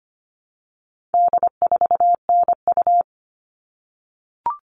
Morse code, heard as a keyed tone, about 25 wpm, 700 Hz. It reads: D4NU
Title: All Cut Numbers - 4-Digit 25wpm